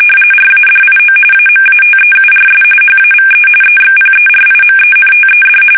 RTTY, about 45.45 baud. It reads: RADIO AMATEURS HAVE EXPERIMENTED
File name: fsk850 / teletype-RTTY